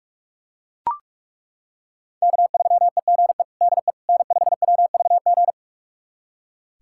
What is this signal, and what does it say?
Morse code, heard as a keyed tone, about 41 wpm, 700 Hz. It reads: K3EGE DE N5WVG
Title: USA Callsign DE Callsign - 5 Characters - 45wpm